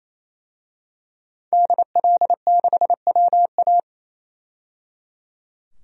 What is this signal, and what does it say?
Morse code, heard as a keyed tone, about 28 wpm, 700 Hz. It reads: DL6WA
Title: DXCC-100 Call Signs - 5 Characters - 28wpm